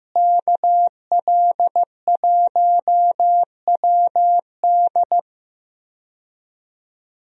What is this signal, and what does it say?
Morse code, heard as a keyed tone, about 15 wpm, 700 Hz. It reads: KL1WD
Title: USA Call Signs - 5 Characters - Repeated 2x - 15wpm